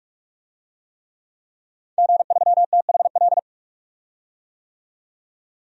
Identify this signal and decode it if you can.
Morse code, heard as a keyed tone, about 46 wpm, 700 Hz. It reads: G3THL
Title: DXCC-100 Call Signs - 5 Characters - 45wpm